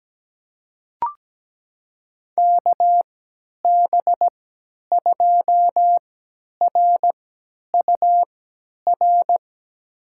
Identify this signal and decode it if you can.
Morse code, heard as a keyed tone, about 17 wpm, spaced as for 10 wpm, 700 Hz. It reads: KB2RUR
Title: USA Call Signs - 6 Characters - 3x Character Spacing - 17wpm